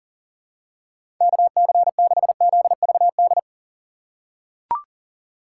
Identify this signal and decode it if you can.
Morse code, heard as a keyed tone, about 40 wpm, 700 Hz. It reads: KC6ZVD
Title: USA Call Signs - 6 Characters - 3x Character Spacing - 40wpm